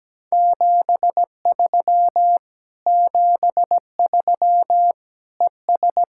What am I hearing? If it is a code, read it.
Morse code, heard as a keyed tone, about 17 wpm, 700 Hz. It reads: 73 73 ES
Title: Speed-Racing Common QSO Phrases - 17wpm